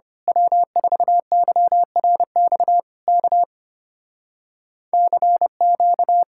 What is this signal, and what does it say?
Morse code, heard as a keyed tone, about 24 wpm, 700 Hz. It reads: W4YRX K CQ
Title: Speed-Racing Common QSO Phrases Encore - 25wpm